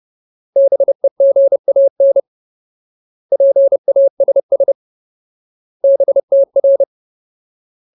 Morse code, heard as a keyed tone, about 30 wpm, 550 Hz. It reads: BEGAN PASS BTR
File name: Ultimate Word List / 30WPM Ultimate